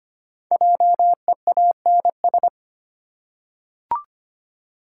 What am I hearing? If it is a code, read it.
Morse code, heard as a keyed tone, about 26 wpm, 700 Hz. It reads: JEANS
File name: practice / 25wpm / Easy Words with Hints - 25wpm